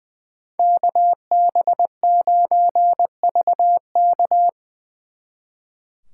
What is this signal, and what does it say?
Morse code, heard as a keyed tone, about 23 wpm, 700 Hz. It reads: KB9VK
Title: Speed Racing - USA 2x2 Call Signs 20wpm